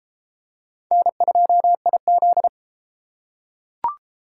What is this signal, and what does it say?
Morse code, heard as a keyed tone, about 32 wpm, 700 Hz. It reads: N2IZ